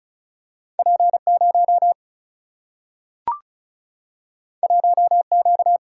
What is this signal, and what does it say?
Morse code, heard as a keyed tone, about 36 wpm, 700 Hz. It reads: P0 1Q
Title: Single-Letter Single-Number 35wpm